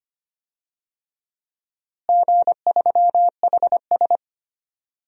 Morse code, heard as a keyed tone, about 25 wpm, 700 Hz. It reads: G3HS
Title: DXCC-100 Call Signs - 4 Characters - Repeated 3x - 25wpm